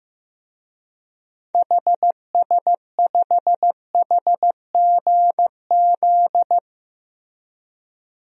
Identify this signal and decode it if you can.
Morse code, heard as a keyed tone, about 15 wpm, 700 Hz. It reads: HS5HGZ
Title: DXCC-100 Call Signs - 6 Characters - Repeated 3x - 15wpm